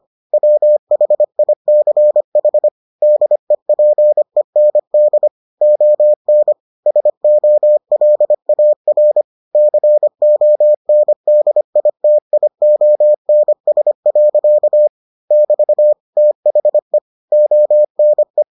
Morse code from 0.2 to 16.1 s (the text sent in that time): WHICH DEPEND ON SOLAR CONDITIONS. <BT>